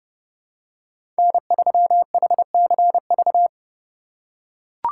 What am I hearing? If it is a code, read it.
Morse code, heard as a keyed tone, about 30 wpm, 700 Hz. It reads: N3HCV
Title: USA Call Signs - 5 Characters - 3x Character Spacing - 30wpm